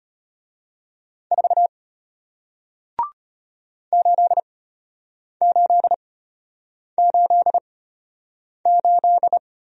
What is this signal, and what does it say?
Morse code, heard as a keyed tone, about 33 wpm, spaced as for 8 wpm, 700 Hz. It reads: 4 8888